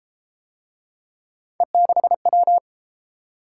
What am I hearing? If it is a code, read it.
Morse code, heard as a keyed tone, about 32 wpm, 700 Hz. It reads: E6W